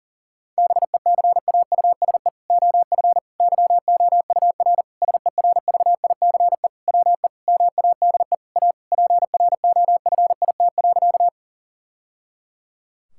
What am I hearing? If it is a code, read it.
Morse code, heard as a keyed tone, about 37 wpm, 700 Hz. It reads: BECAUSE OF YOUR SERVICE WE MADE A PROFIT.
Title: Sentences from Top 1000 Words - Review Part 5 - 40wpm